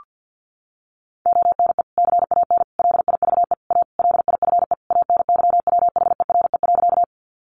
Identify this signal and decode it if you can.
Morse code, heard as a keyed tone, about 50 wpm, 700 Hz. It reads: ONE CAN LIVE A LIFE ANYWHERE.
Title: Sentences from Top 1400 Words 50wpm